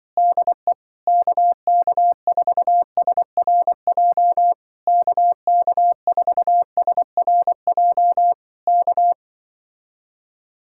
Morse code, heard as a keyed tone, about 24 wpm, 700 Hz. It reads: DE KK4SRJ KK4SRJ K
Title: Speed-Racing Common QSO Phrases Encore - 20wpm